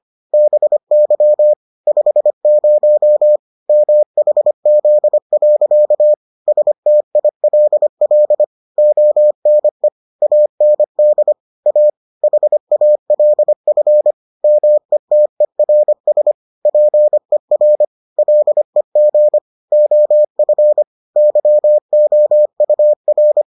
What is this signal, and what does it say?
Morse code, heard as a keyed tone, about 25 wpm, 600 Hz. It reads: BY 50 MHZ. STILL ONE AND A HALF METERS PER LEG OF YOUR